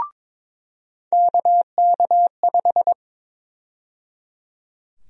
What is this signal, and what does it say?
Morse code, heard as a keyed tone, about 22 wpm, 700 Hz. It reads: KK5
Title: USA Call Sign Prefixes - 22wpm